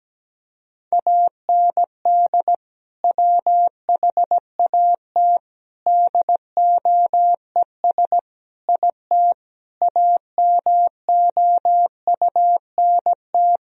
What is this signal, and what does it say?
Morse code, heard as a keyed tone, about 17 wpm, 700 Hz. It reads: AND WHAT DOES IT AMOUNT
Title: Sentences from Top 1000 Words - Review Part 4 - 17wpm